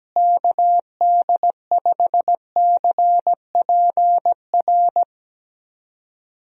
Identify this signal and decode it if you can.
Morse code, heard as a keyed tone, about 17 wpm, 700 Hz. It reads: KD5CPR
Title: USA Call Signs - 6 Characters - Repeated 2x - 17wpm